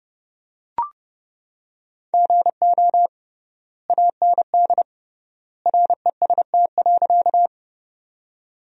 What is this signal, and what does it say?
Morse code, heard as a keyed tone, about 30 wpm, 700 Hz. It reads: GO AND REST.
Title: 3x Word-Spacing - Sentences from Top 500 Words - 30wpm